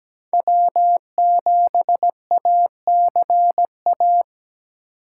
Morse code, heard as a keyed tone, about 17 wpm, 700 Hz. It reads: W7ACA
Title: USA Call Signs - 2x Character Spacing - 17wpm